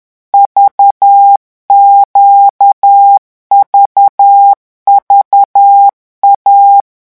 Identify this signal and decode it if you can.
Morse code, heard as a keyed tone, about 10 wpm, 800 Hz. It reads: VQVVA
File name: morse mp3 / revision3-vitesse 10 mots